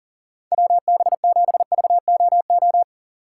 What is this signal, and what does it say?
Morse code, heard as a keyed tone, about 40 wpm, 700 Hz. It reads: WB7VOO
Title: USA Call Signs - 4x Character Spacing - 40wpm